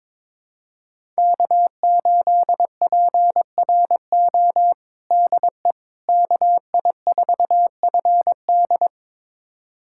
Morse code, heard as a keyed tone, about 22 wpm, 700 Hz. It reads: K8PRO DE KI4FD
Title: USA Callsign DE Callsign - 5 Characters - 22wpm